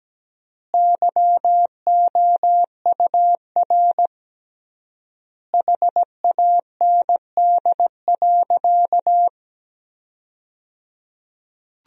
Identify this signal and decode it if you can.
Morse code, heard as a keyed tone, about 17 wpm, 700 Hz. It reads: YOUR HAND.
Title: 3x Word-Spacing - Sentences from Top 400 Words - 17wpm